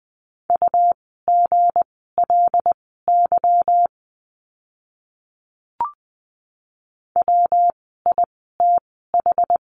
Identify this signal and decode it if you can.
Morse code, heard as a keyed tone, about 20 wpm, spaced as for 15 wpm, 700 Hz. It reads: UGLY WITH
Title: 160 Lesson 26f - Letter g - Words 20wpm